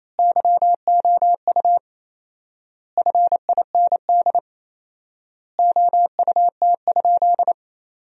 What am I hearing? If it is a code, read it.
Morse code, heard as a keyed tone, about 28 wpm, 700 Hz. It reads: YOU FIND OUT?
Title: 4x Word-Spacing - Sentences from Top 100 Words - 28wpm